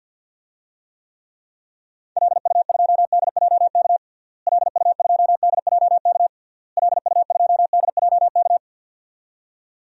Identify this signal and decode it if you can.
Morse code, heard as a keyed tone, about 52 wpm, 700 Hz. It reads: LU2DJK LU2DJK LU2DJK
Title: DXCC-100 Call Signs - 6 Characters - Repeated 3x - 50wpm